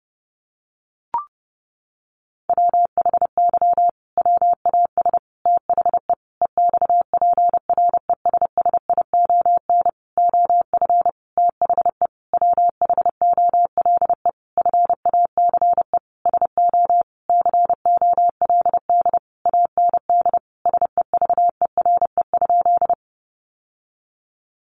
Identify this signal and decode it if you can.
Morse code, heard as a keyed tone, about 30 wpm, 700 Hz. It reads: WHY WAS THE EXPRESSION OF THE WHOLE FACE SO COLD AND SEVERE?
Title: Sentences from Top 1400 Words 30wpm